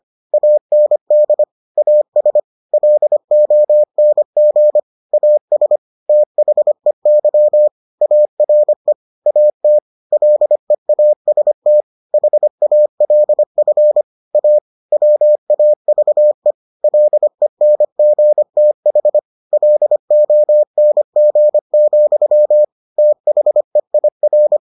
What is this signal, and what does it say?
Morse code, heard as a keyed tone, about 25 wpm, 600 Hz. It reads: AND AS LONG AS THEY ARE AT LEAST HALF A WAVE LENGTH LONG, THEIR